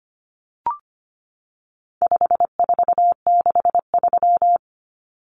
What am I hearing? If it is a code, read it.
Morse code, heard as a keyed tone, about 25 wpm, 700 Hz. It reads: 5463
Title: Numbers - 4-Digit 25wpm